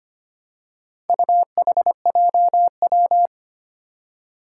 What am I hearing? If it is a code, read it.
Morse code, heard as a keyed tone, about 25 wpm, 700 Hz. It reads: UHJW
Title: Four Challenging Letter-Number Combinations 25wpm